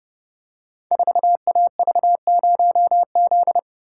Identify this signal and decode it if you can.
Morse code, heard as a keyed tone, about 30 wpm, 700 Hz. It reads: 4AV0Z
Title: Code Groups - 5 Letter:Numbers - 30wpm